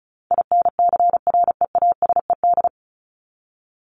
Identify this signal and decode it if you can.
Morse code, heard as a keyed tone, about 34 wpm, 700 Hz. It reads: INCREASED
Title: ED Suffix - Top 100 Words with Base Words 35wpm